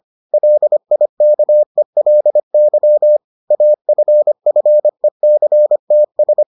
Morse code, heard as a keyed tone, about 25 wpm, 600 Hz. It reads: LIKELY AFFECTS